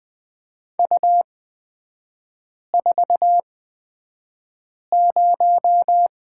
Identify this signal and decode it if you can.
Morse code, heard as a keyed tone, about 20 wpm, spaced as for 5 wpm, 700 Hz. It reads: U40
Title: 255 Lesson 41e - Period - Three Characters 20wpm